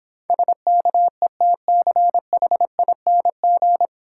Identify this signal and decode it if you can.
Morse code, heard as a keyed tone, about 25 wpm, 700 Hz. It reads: SKETCHING
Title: Easy Words - Speed-Racing with Hints - 20wpm